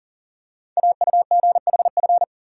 Morse code, heard as a keyed tone, about 39 wpm, 690 Hz. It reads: AUGHF
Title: Code Groups - 5 Letters - 40wpm